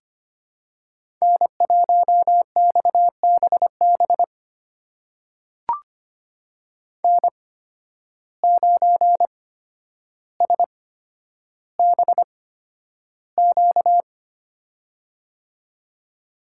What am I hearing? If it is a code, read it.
Morse code, heard as a keyed tone, about 25 wpm, spaced as for 11 wpm, 700 Hz. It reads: N1XBB N9SBQ
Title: USA Call Signs - 5 Characters - 8x Character Spacing - 25wpm